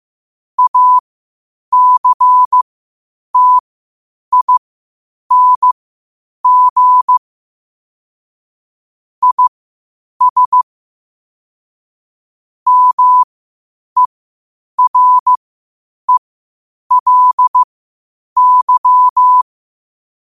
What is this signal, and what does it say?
Morse code, heard as a keyed tone, about 15 wpm, spaced as for 8 wpm, 1000 Hz. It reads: ACTING IS MERELY
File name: qotdfiles / 8wpm